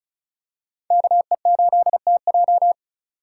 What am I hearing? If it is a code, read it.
Morse code, heard as a keyed tone, about 36 wpm, 700 Hz. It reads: KE8TJ
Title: USA Call Signs - 5 Characters - 3x Character Spacing - 35wpm